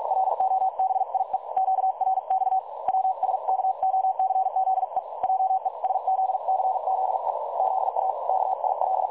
Morse code, heard as a keyed tone, about 46 wpm, 760 Hz. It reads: ROBIE QRX WSKOCZE JESZCZE ZA KIL
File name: dlazaawansowanych / nagraniazpasma / websdr recording 3565.0kHz